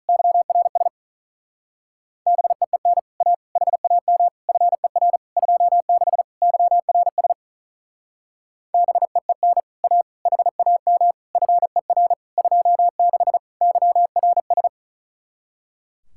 Morse code, heard as a keyed tone, about 44 wpm, 700 Hz. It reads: YRS BEEN A HAM FER 26 YRS BEEN A HAM FER 26 YRS